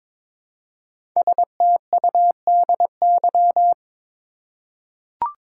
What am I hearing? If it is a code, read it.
Morse code, heard as a keyed tone, about 22 wpm, 700 Hz. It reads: STUDY
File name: 5 Letter Words - First 3 Letters Given - 22wpm